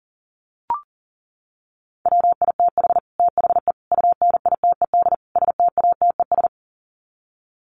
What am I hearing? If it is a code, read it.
Morse code, heard as a keyed tone, about 40 wpm, 700 Hz. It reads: WITH THE UNITED STATES
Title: Sets of 4 Words 40wpm